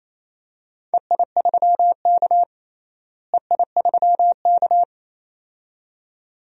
Morse code, heard as a keyed tone, about 28 wpm, 700 Hz. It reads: EI3K EI3K
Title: DXCC-100 Call Signs - 4 Characters - Repeated 2x - 28wpm